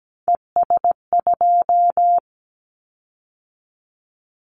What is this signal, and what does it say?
Morse code, heard as a keyed tone, about 17 wpm, 700 Hz. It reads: ES2
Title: Callsign Prefixes for Top 100 DXCC - 17wpm